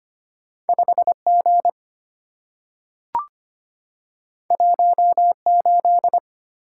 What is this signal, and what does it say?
Morse code, heard as a keyed tone, about 25 wpm, 700 Hz. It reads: 5G 18
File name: Two Letter-Number Combinations - 25wpm